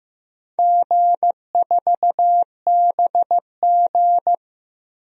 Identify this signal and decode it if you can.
Morse code, heard as a keyed tone, about 15 wpm, 700 Hz. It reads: G4BG
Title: DXCC-100 Call Signs - 4 Characters - 15wpm